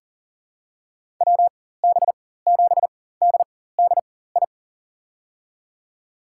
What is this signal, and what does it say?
Morse code, heard as a keyed tone, about 40 wpm, spaced as for 19 wpm, 700 Hz. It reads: WB7DDI
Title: USA Call Signs - 6 Characters - 4x Character Spacing - 40wpm